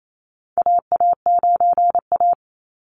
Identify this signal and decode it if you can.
Morse code, heard as a keyed tone, about 28 wpm, 700 Hz. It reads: AA9A
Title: Call Signs - 28wpm